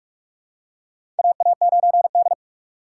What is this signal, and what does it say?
Morse code, heard as a keyed tone, about 41 wpm, 700 Hz. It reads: AA9D
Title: USA Call Signs - 2x Character Spacing - 45wpm